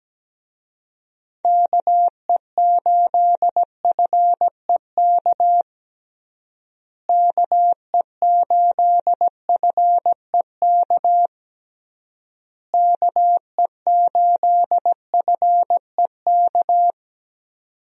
Morse code, heard as a keyed tone, about 17 wpm, 700 Hz. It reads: KE8FEK KE8FEK KE8FEK
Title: USA Call Signs - 6 Characters - Repeated 3x - 17wpm